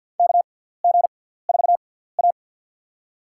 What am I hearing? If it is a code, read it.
Morse code, heard as a keyed tone, about 52 wpm, spaced as for 17 wpm, 700 Hz. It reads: KG4A
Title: USA Call Signs - 4 Characters - 6x Character Spacing - 50wpm